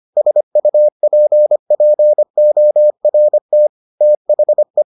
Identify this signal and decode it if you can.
Morse code, heard as a keyed tone, about 25 wpm, 600 Hz. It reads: SUPPORT THE